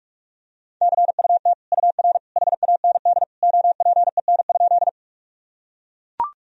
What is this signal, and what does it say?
Morse code, heard as a keyed tone, about 41 wpm, 700 Hz. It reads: CUT UR HAND OPEN?
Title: Sentences from Top 300 Words and Abbreviations 45wpm